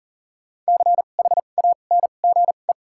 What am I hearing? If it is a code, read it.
Morse code, heard as a keyed tone, about 37 wpm, spaced as for 28 wpm, 700 Hz. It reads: CHANGE
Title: 2x Character Spacing - Top 200 Words - 40wpm